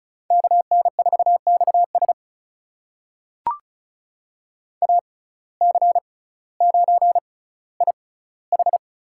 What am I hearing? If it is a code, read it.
Morse code, heard as a keyed tone, about 36 wpm, spaced as for 18 wpm, 700 Hz. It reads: KN4XS AC9IH
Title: USA Call Signs - 5 Characters - 6x Character Spacing - 35wpm